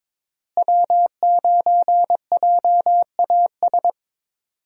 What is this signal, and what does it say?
Morse code, heard as a keyed tone, about 22 wpm, 700 Hz. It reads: W9JAS
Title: USA Call Signs - 5 Characters - Repeated 3x - 22wpm